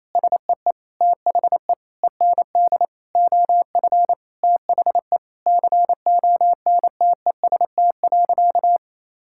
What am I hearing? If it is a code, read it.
Morse code, heard as a keyed tone, about 27 wpm, 700 Hz. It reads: SEE THE END OF THE CONTEST.
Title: Sentences from Top 1600 Words 28wpm